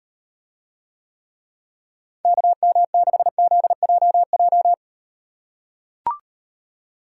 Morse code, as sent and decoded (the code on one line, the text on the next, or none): -.- -- -.... --.. .--- .---
KM6ZJJ